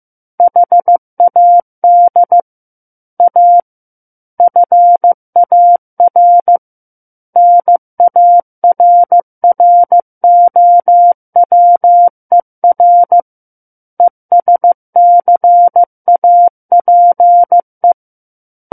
Morse code, heard as a keyed tone, about 15 wpm, 700 Hz. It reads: HAD A FAR NARROWER ESCAPE
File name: Book One / 06-Chapter 6 - 15 WPM